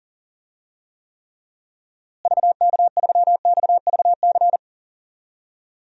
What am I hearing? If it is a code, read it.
Morse code, heard as a keyed tone, about 38 wpm, 700 Hz. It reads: VK3XVC
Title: DXCC-100 Call Signs - 6 Characters - Repeated 3x - 40wpm